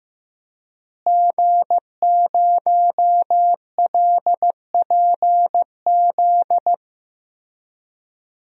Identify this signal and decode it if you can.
Morse code, heard as a keyed tone, about 15 wpm, 700 Hz. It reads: G0LPZ
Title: DXCC-100 Call Signs - 5 Characters - Repeated 3x - 15wpm